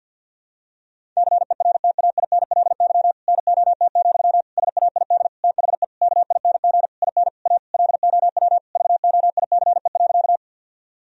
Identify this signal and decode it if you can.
Morse code, heard as a keyed tone, about 51 wpm, 700 Hz. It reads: CERTAINLY NOT, SAID THE KING IN A LOW VOICE.
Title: Sentences from Top 1000 Words - Review Part 1 - 50wpm